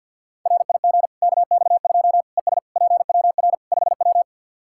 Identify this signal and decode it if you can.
Morse code, heard as a keyed tone, about 49 wpm, 700 Hz. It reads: RIG KX2 ES PWR 5W
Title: K1USN SST QSO Practice 50wpm